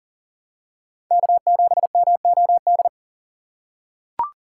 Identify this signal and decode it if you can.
Morse code, heard as a keyed tone, about 40 wpm, 700 Hz. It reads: K7MOD
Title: USA Call Signs - 5 Characters - 3x Character Spacing - 40wpm